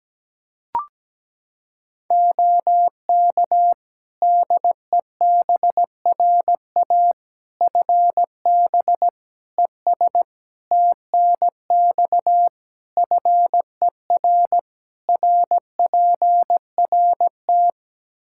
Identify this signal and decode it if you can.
Morse code, heard as a keyed tone, about 17 wpm, 700 Hz. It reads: OK DEBRA FB ES TNX FER RPRT
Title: Practice QSO - Exchange 2 - Info - 17wpm